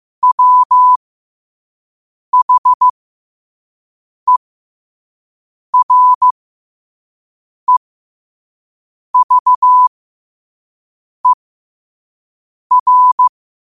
Morse code, heard as a keyed tone, about 15 wpm, spaced as for 5 wpm, 1000 Hz. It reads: WHEREVER
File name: qotdfiles / 5wpm